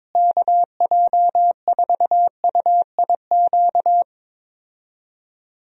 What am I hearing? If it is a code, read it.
Morse code, heard as a keyed tone, about 21 wpm, 700 Hz. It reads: KJ4UIQ
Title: USA Call Signs - 6 Characters - Repeated 2x - 22wpm